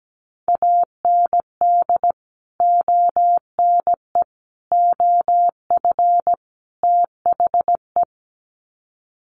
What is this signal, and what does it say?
Morse code, heard as a keyed tone, about 17 wpm, 700 Hz. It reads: AND ONE OF THE
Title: Sets of 4 Words 17wpm